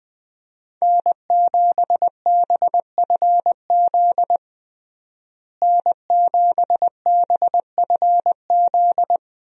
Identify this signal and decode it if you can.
Morse code, heard as a keyed tone, about 21 wpm, 700 Hz. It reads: N7BFZ N7BFZ
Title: USA Call Signs - 5 Characters - Repeated 2x - 20wpm